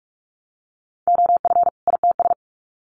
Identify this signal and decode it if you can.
Morse code, heard as a keyed tone, about 46 wpm, 700 Hz. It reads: OF ITS
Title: Sets of 2 Words 45wpm